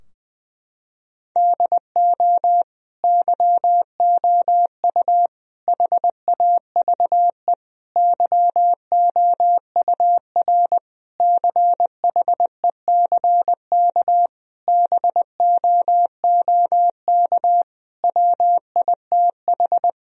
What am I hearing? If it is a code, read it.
Morse code, heard as a keyed tone, about 20 wpm, 700 Hz. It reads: DO YOU HAVE YOUR CHECK BOOK WITH
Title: Sentences from Top 1000 Words - Review Part 3 - 20wpm